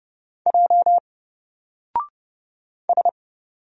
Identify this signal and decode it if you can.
Morse code, heard as a keyed tone, about 30 wpm, 700 Hz. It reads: J S
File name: Single-Letter-Number-30wpm